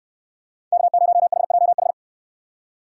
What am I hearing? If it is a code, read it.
Morse code, heard as a keyed tone, about 63 wpm, 700 Hz. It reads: K0HJH